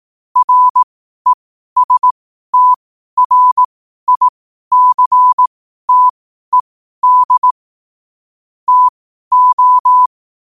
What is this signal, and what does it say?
Morse code, heard as a keyed tone, about 18 wpm, spaced as for 12 wpm, 1000 Hz. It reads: RESTRICTED TO
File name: qotdfiles / 12wpm